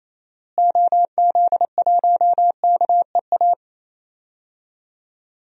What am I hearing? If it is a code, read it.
Morse code, heard as a keyed tone, about 28 wpm, 700 Hz. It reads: OZ1KEA
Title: DXCC-100 Call Signs - 6 Characters - Repeated 2x - 28wpm